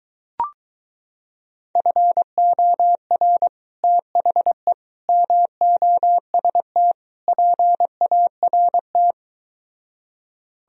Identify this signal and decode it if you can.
Morse code, heard as a keyed tone, about 23 wpm, 700 Hz. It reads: FOR THE MOST PART